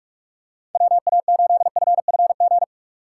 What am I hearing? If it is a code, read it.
Morse code, heard as a keyed tone, about 41 wpm, 700 Hz. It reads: WA8FFG